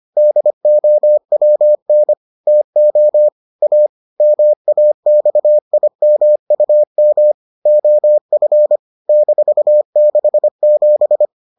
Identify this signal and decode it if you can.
Morse code, heard as a keyed tone, about 25 wpm, 600 Hz. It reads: DOWN TO A MAXIMUM OF -67